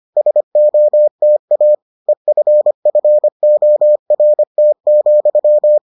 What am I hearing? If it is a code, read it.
Morse code, heard as a keyed tone, about 25 wpm, 600 Hz. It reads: SOTA EFFORT,